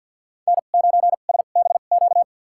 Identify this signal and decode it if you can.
Morse code, heard as a keyed tone, about 53 wpm, spaced as for 37 wpm, 700 Hz. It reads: N9SBQ
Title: USA Call Signs - 5 Characters - 2x Character Spacing - 50wpm